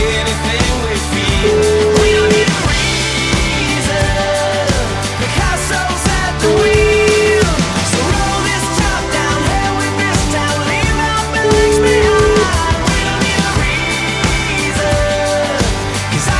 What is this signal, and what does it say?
Ringback tone (Europe, 425 Hz), with noise.